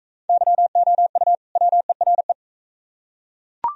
Morse code, heard as a keyed tone, about 39 wpm, 700 Hz. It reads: YOU WERE